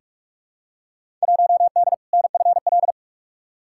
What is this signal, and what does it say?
Morse code, heard as a keyed tone, about 47 wpm, 700 Hz. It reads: 1D NFL
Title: Field Day Exchanges - 45wpm